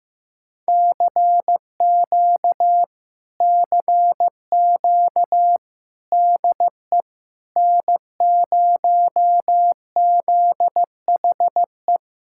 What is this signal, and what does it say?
Morse code, heard as a keyed tone, about 15 wpm, 700 Hz. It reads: CQ CQ DE N0ZHE
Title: Speed-Racing Common QSO Phrases Encore - 15wpm